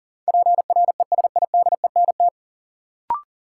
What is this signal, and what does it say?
Morse code, heard as a keyed tone, about 40 wpm, 700 Hz. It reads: PRESIDENT
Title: Sets of 2 Words - New Word Drill - 40wpm